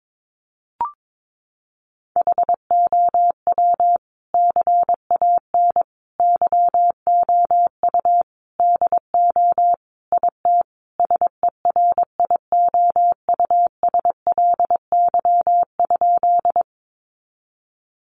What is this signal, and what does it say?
Morse code, heard as a keyed tone, about 22 wpm, 700 Hz. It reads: HOW CAN YOU DO IT SERIOUSLY?
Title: Sentences from Top 1200 Words 22wpm